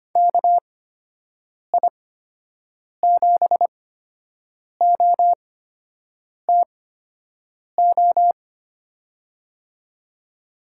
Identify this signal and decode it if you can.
Morse code, heard as a keyed tone, about 25 wpm, spaced as for 7 wpm, 700 Hz. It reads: KI7OTO
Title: USA Call Signs - 6 Characters - 8x Character Spacing - 25wpm